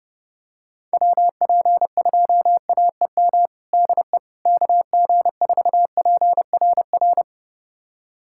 Morse code, heard as a keyed tone, about 30 wpm, 700 Hz. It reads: WP2AEM DE KG4PRR
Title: USA Callsign DE Callsign - 6 Characters - 30wpm